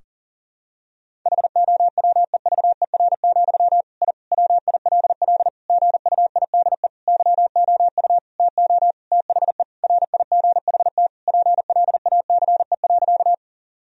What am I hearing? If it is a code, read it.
Morse code, heard as a keyed tone, about 37 wpm, 700 Hz. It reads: HOWEVER, I WILL GUIDE YOU TO THE RIGHT PLACE.
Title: Sentences from Top 1000 Words - Review Part 3 - 40wpm